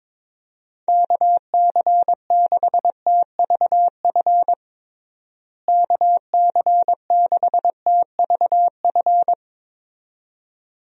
Morse code, heard as a keyed tone, about 22 wpm, 700 Hz. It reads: KC6TVF KC6TVF
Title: USA Call Signs - 6 Characters - Repeated 2x - 22wpm